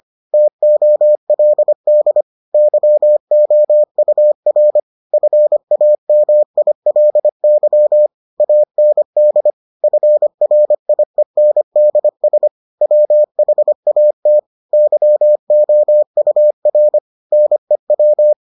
Morse code, heard as a keyed tone, about 25 wpm, 600 Hz. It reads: TOLD YOUR FAMILY AND FRIENDS WHAT YOUR NEW